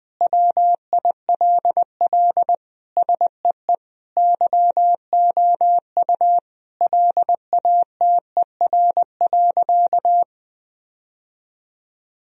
Morse code, heard as a keyed tone, about 20 wpm, 700 Hz. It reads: WILL SEE YOU LATER.
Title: Sentences from Top 1000 Words - Review Part 4 - 20wpm